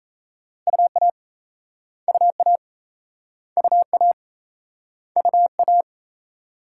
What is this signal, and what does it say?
Morse code, heard as a keyed tone, about 39 wpm, 700 Hz. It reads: UA UA UA UA